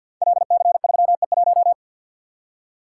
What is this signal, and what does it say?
Morse code, heard as a keyed tone, about 50 wpm, 700 Hz. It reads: LC3E1
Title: Code Groups - 5 Letter:Numbers - 50wpm